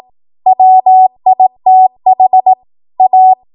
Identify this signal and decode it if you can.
Morse code, heard as a keyed tone, about 18 wpm, 750 Hz. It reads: WITH A
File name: Morse / 18WPM